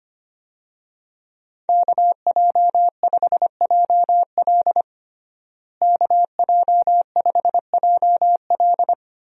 Morse code, heard as a keyed tone, about 25 wpm, 700 Hz. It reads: KJ5JL KJ5JL
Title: USA Call Signs - 5 Characters - Repeated 3x - 25wpm